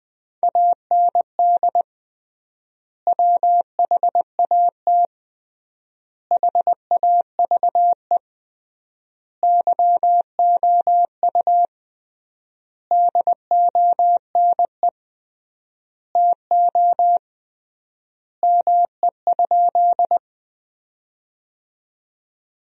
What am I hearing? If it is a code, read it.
Morse code, heard as a keyed tone, about 20 wpm, 700 Hz. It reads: AND WHAT HAVE YOU DONE TO ME?
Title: 3x Word-Spacing - Sentences from Top 500 Words - 20wpm